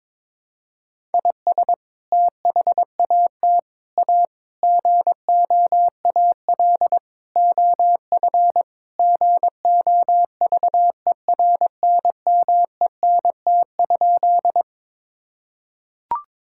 Morse code, heard as a keyed tone, about 22 wpm, 700 Hz. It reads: IS THAT A GOAL OF GOVERNMENT?
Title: Sentences from Top 1000 Words - Review Part 5 - 22wpm